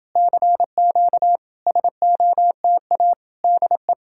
Morse code, heard as a keyed tone, about 28 wpm, 700 Hz. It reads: CQ SOTA DE